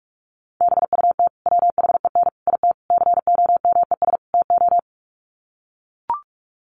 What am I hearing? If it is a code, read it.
Morse code, heard as a keyed tone, about 44 wpm, 700 Hz. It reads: BUT WHEN IT COMES TO
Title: Sets of 5 Words 45wpm